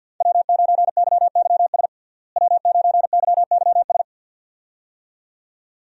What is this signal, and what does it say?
Morse code, heard as a keyed tone, about 47 wpm, 700 Hz. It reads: W9YYS W9YYS
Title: USA Call Signs - 5 Characters - Repeated 2x - 50wpm